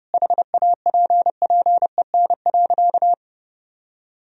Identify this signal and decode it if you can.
Morse code, heard as a keyed tone, about 30 wpm, 700 Hz. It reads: HAPPEN.
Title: Sentences from Top 1000 Words - Review Part 4 - 30wpm